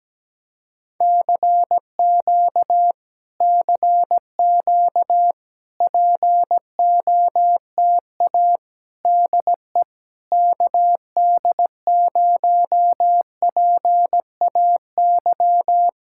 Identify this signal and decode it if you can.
Morse code, heard as a keyed tone, about 17 wpm, 700 Hz. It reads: CQ CQ POTA DE KD0PAY
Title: Speed-Racing Common QSO Phrases - 17wpm